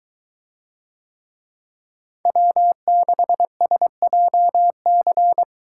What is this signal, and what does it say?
Morse code, heard as a keyed tone, about 23 wpm, 700 Hz. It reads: W6SJC